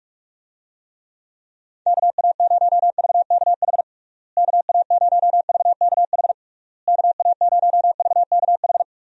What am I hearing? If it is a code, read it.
Morse code, heard as a keyed tone, about 41 wpm, 700 Hz. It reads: KA0VKH KA0VKH KA0VKH
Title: USA Call Signs - 6 Characters - Repeated 3x - 45wpm